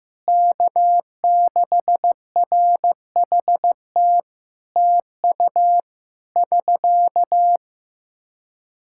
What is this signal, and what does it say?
Morse code, heard as a keyed tone, about 15 wpm, 700 Hz. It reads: K6RHT TU <SK>
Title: Practice QSO - Exchange 4 - Ending - 15wpm